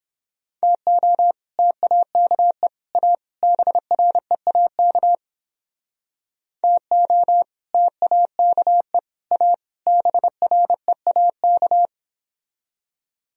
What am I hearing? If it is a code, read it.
Morse code, heard as a keyed tone, about 31 wpm, 700 Hz. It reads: TO TAKE A BREAK TO TAKE A BREAK